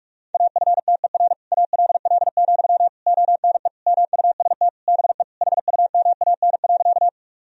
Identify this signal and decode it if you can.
Morse code, heard as a keyed tone, about 47 wpm, 700 Hz. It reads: AFTER ALL, ONE MUST BE HUMAN.
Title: Sentences from Top 1000 Words - Review Part 3 - 45wpm